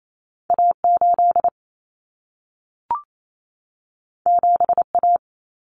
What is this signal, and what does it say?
Morse code, heard as a keyed tone, about 28 wpm, 700 Hz. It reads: A8 7A
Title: Common Cut Numbers - 2-Digit 28wpm